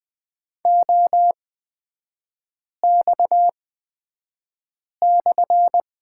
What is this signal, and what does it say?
Morse code, heard as a keyed tone, about 20 wpm, spaced as for 5 wpm, 700 Hz. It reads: OX/
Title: 250 Lesson 40e - BK - Three Characters 20wpm